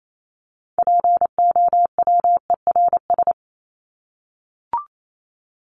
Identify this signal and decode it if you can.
Morse code, heard as a keyed tone, about 28 wpm, 700 Hz. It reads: POWERS
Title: ES:S Suffix - Top 100 Words with Base Words 28wpm